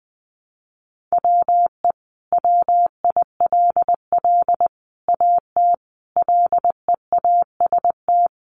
Morse code, heard as a keyed tone, about 19 wpm, 700 Hz. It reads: WE WILL AT LEAST
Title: Sentences from Top 1200 Words 20wpm